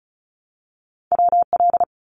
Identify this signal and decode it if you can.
Morse code, heard as a keyed tone, about 34 wpm, 700 Hz. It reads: WL
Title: Abbreviations Repeated 3x - 35wpm